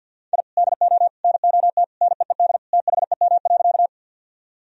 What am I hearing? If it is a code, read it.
Morse code, heard as a keyed tone, about 53 wpm, 700 Hz. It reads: I DO NOT NEED THEM.
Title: Sentences from Top 1000 Words - Review Part 1 - 50wpm